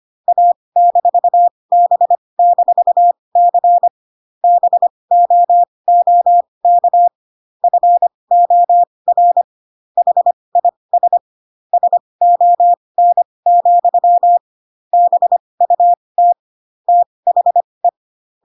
Morse code, heard as a keyed tone, about 25 wpm, spaced as for 20 wpm, 700 Hz. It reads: A-B-C BOOK FOR HIS SON, BUT THE
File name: Pinocchio / Pinocchio ch8 15 25 ik1hge